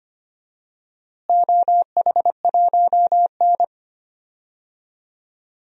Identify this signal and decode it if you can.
Morse code, heard as a keyed tone, about 25 wpm, 700 Hz. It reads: OH1N
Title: DXCC-100 Call Signs - 4 Characters - Repeated 3x - 25wpm